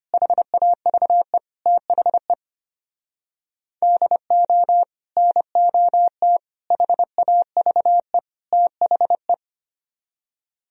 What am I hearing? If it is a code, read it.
Morse code, heard as a keyed tone, about 33 wpm, 700 Hz. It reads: HAVE THE DO NOT HAVE THE